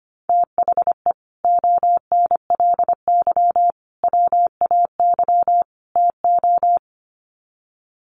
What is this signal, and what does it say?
Morse code, heard as a keyed tone, about 25 wpm, 700 Hz. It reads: THE ONLY WAY TO
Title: Sets of 4 Words 25wpm